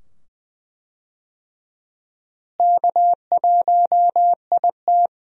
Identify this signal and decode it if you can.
Morse code, heard as a keyed tone, about 20 wpm, 700 Hz. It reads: K1IT
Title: USA Call Signs - 4 Characters - Repeated 2x - 20wpm